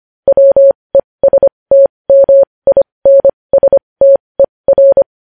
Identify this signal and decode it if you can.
Morse code, heard as a keyed tone, about 25 wpm, spaced as for 20 wpm, 550 Hz. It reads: WESTMINSTER
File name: podcast / News headlines-1x-25x20